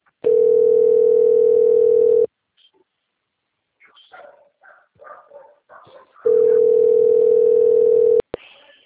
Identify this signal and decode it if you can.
Ringback tone (North America), 440 + 480 Hz.